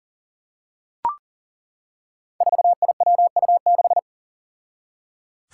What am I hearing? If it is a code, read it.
Morse code, heard as a keyed tone, about 39 wpm, 700 Hz. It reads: 4IWU6
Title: Five Challenging Letter-Number Combinations 40wpm